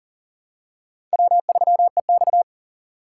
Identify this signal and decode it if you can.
Morse code, heard as a keyed tone, about 40 wpm, 700 Hz. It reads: W3EX
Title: USA Call Signs - 4 Characters - 2x Character Spacing - 40wpm